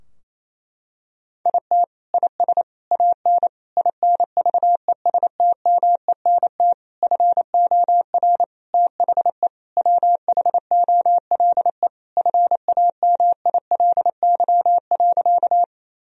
Morse code, heard as a keyed tone, about 28 wpm, 700 Hz. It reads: IT IS AN INVESTMENT FOR THE WHOLE FAMILY.
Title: Sentences from Top 1000 Words - Review Part 4 - 28wpm